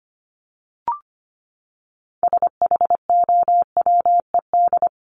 Morse code, heard as a keyed tone, about 25 wpm, 700 Hz. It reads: SHOWED
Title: ED Suffix - Top 100 Words with Base Words 25wpm